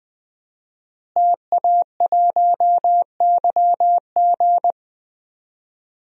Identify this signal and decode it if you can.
Morse code, heard as a keyed tone, about 20 wpm, 700 Hz. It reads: TA1YG
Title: DXCC-100 Call Signs - 5 Characters - Repeated 3x - 20wpm